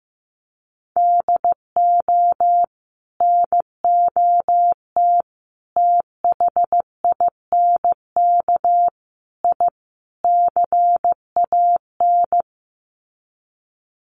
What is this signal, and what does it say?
Morse code, heard as a keyed tone, about 15 wpm, 700 Hz. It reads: DO NOT THINK I CAN
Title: Sets of 5 Words Encore2 15wpm